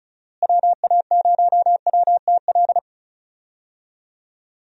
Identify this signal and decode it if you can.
Morse code, heard as a keyed tone, about 34 wpm, 700 Hz. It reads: WA0WTL
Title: USA Call Signs - 6 Characters - Repeated 2x - 35wpm